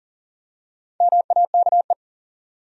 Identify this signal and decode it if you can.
Morse code, heard as a keyed tone, about 40 wpm, 700 Hz. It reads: MAKE
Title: Top 100 Words 40wpm